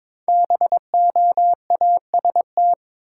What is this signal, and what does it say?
Morse code, heard as a keyed tone, about 22 wpm, 700 Hz. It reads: BOAST
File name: 5 Letter Words - First 3 Letters Given - 22wpm